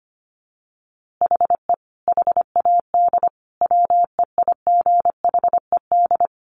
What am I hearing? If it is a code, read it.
Morse code, heard as a keyed tone, about 25 wpm, 700 Hz. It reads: HE HAD WEIGHED
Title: A Sound Of Thunder 25wpm